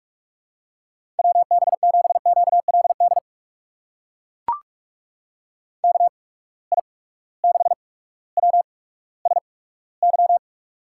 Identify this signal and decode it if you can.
Morse code, heard as a keyed tone, about 44 wpm, spaced as for 19 wpm, 700 Hz. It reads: WB7QLD KI6WSY
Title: USA Call Signs - 6 Characters - 8x Character Spacing - 45wpm